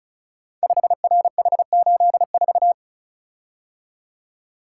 Morse code, heard as a keyed tone, about 36 wpm, 690 Hz. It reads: 5RH84
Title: Code Groups - 5 Letter:Numbers - 35wpm